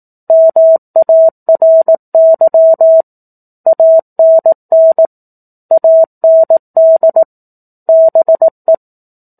Morse code, heard as a keyed tone, about 18 wpm, 650 Hz. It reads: MARY ANN AND BE
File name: 04-Chapter-04 18 WPM